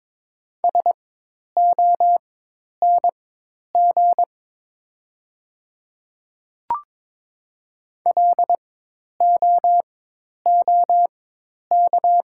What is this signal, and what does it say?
Morse code, heard as a keyed tone, about 22 wpm, spaced as for 10 wpm, 700 Hz. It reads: SONG LOOK
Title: 4x Character Spacing - Top 500 Words - 22wpm